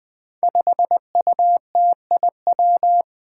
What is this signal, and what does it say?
Morse code, heard as a keyed tone, about 20 wpm, 700 Hz. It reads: 5UTIW
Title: Five Challenging Letter-Number Combinations 20wpm